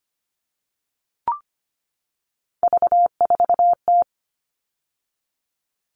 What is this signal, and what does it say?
Morse code, heard as a keyed tone, about 25 wpm, 700 Hz. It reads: V4T
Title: All Cut Numbers - 3-Digit 25wpm